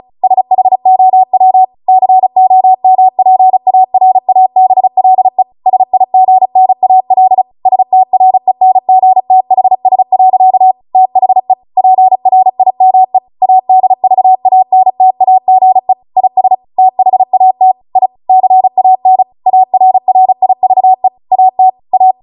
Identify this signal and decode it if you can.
Morse code, heard as a keyed tone, about 36 wpm, 750 Hz. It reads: SHOW COMPARABLE SIGNAL STRENGTHS. THE PRIME ADVANTAGE IS THAT I CAN ARRIVE AT A
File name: Morse / 35WPM